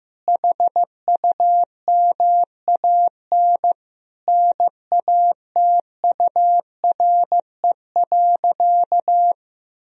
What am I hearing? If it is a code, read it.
Morse code, heard as a keyed tone, about 15 wpm, 700 Hz. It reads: HUMAN NATURE.
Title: Sentences from Top 1000 Words - Review Part 3 - 15wpm